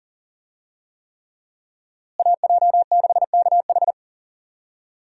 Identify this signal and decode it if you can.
Morse code, heard as a keyed tone, about 40 wpm, 700 Hz. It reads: AJ6KH
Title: USA Call Signs - 5 Characters - Repeated 3x - 40wpm